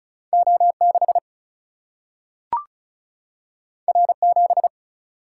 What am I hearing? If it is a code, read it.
Morse code, heard as a keyed tone, about 36 wpm, 700 Hz. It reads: O6 R7
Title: Single-Letter Single-Number 35wpm